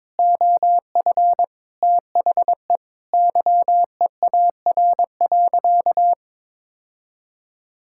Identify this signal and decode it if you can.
Morse code, heard as a keyed tone, about 22 wpm, 700 Hz. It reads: OF THE YEAR.
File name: practice / 22wpm / Sentences with 8-Word Phrases - 22wpm